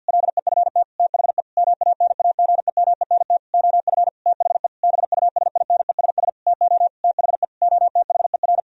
Morse code, heard as a keyed tone, about 50 wpm, 700 Hz. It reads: LEFT THE MANAGEMENT OF THE BUSINESS TO THE OTHER
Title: Sentences from Top 1000 Words - Review Part 4 - 50wpm